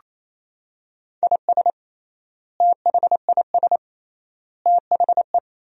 Morse code, heard as a keyed tone, about 27 wpm, 700 Hz. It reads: IS THIS THE
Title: 3x Word-Spacing - Sentences from Top 200 Words - 28wpm